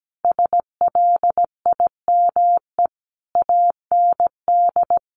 Morse code, heard as a keyed tone, about 17 wpm, 700 Hz. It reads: SLIME AND
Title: A Sound Of Thunder 17wpm